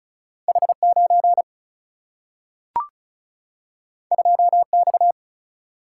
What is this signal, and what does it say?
Morse code, heard as a keyed tone, about 35 wpm, 700 Hz. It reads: H9 2X
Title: Single-Letter Single-Number 35wpm